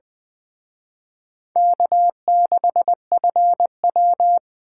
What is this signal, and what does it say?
Morse code, heard as a keyed tone, about 23 wpm, 700 Hz. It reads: K6FW
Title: Speed Racing - USA 1x2 Call Signs 20wpm